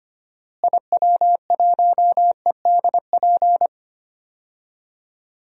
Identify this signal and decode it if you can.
Morse code, heard as a keyed tone, about 25 wpm, 700 Hz. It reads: IW1EDP
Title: DXCC-100 Call Signs - 6 Characters - Repeated 3x - 25wpm